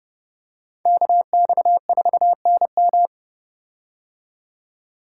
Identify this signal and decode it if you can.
Morse code, heard as a keyed tone, about 31 wpm, 700 Hz. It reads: KX4NM
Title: K1USN SST QSO Practice - 30wpm